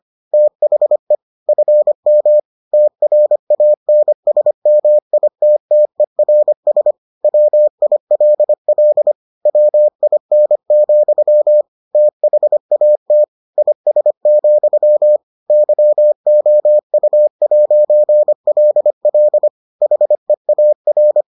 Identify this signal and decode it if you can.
Morse code, heard as a keyed tone, about 25 wpm, 600 Hz. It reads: THE FM TRANSMITTERS WILL WIN, THAT IS, YOU'LL HEAR